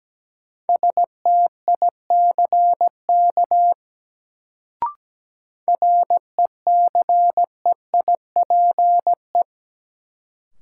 Morse code, heard as a keyed tone, about 17 wpm, 700 Hz. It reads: STICK RECEIPE
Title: Sentences from Top 1200 Words - New Word Drill Alt - 17wpm